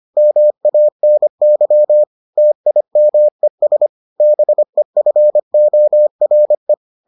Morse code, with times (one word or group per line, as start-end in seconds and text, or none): 0.2-2.0 MANY
2.4-3.9 TIMES
4.2-6.7 BEFORE